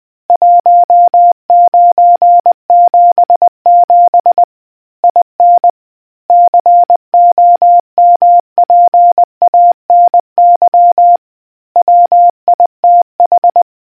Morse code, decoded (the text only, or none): 1977 IN COMPANY WITH